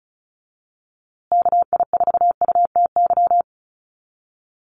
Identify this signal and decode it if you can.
Morse code, heard as a keyed tone, about 34 wpm, 700 Hz. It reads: KI4UTY
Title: USA Call Signs Repeated 2x 35wpm